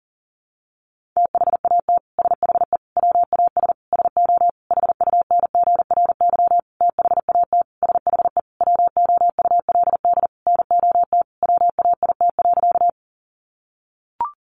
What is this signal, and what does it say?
Morse code, heard as a keyed tone, about 40 wpm, 700 Hz. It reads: THAT SHE WAS SO HUNGRY THAT SHE WOULD NOT WAIT.
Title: Sentences from Top 1400 Words 40wpm